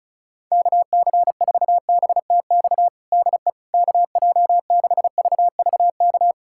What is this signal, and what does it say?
Morse code, heard as a keyed tone, about 36 wpm, 700 Hz. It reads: KC4BTX DE KJ6VVK
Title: USA Callsign DE Callsign - 6 Characters - 35wpm